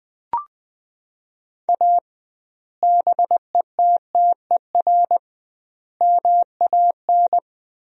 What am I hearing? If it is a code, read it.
Morse code, heard as a keyed tone, about 20 wpm, 700 Hz. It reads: A BETTER MAN
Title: 2x Word-Spacing - Sentences from Top 400 Words - 20wpm